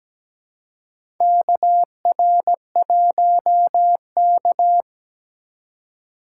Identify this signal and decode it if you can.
Morse code, heard as a keyed tone, about 17 wpm, 700 Hz. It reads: KR1K
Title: USA Call Signs - 4 Characters - 2x Character Spacing - 17wpm